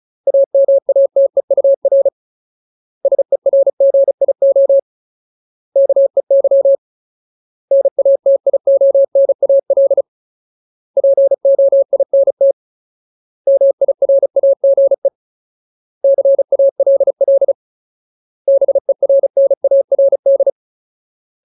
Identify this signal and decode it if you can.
Morse code, heard as a keyed tone, about 37 wpm, 550 Hz. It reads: AMATEUR SERGIO KEY NATIONAL POINT MIRAGE CALL BERNARD
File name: Ultimate Word List / 35WPM Ultimate